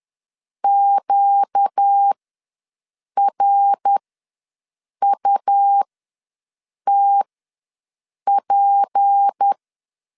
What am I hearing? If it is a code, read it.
Morse code, heard as a keyed tone, about 10 wpm, spaced as for 6 wpm, 800 Hz. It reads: QRUTP
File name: LEZIONI / 6 PKX0 / 6 P ws 10wpm 3751